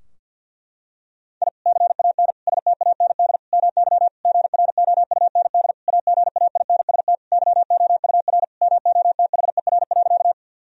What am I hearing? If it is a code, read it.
Morse code, heard as a keyed tone, about 52 wpm, 700 Hz. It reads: I CAN STAND MY GROUND AGAINST YOUR MOTHER.
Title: Sentences from Top 1000 Words - Review Part 2 - 50wpm